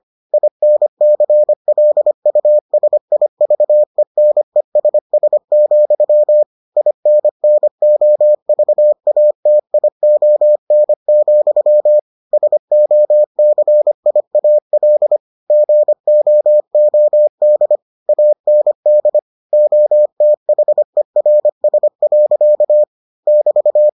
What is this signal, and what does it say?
Morse code, heard as a keyed tone, about 25 wpm, 600 Hz. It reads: INCLUSIVENESS, INNOVATION, SOCIAL GOOD AND OTHERS. <BT>